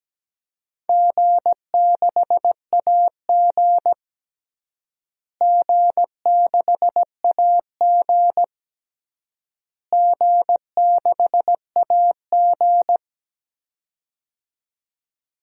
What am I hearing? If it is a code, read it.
Morse code, heard as a keyed tone, about 17 wpm, 700 Hz. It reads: G6AG G6AG G6AG
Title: DXCC-100 Call Signs - 4 Characters - Repeated 3x - 17wpm